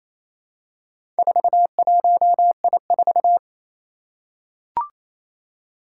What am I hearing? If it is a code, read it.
Morse code, heard as a keyed tone, about 28 wpm, 700 Hz. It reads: 41I4
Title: Four Challenging Letter-Number Combinations 28wpm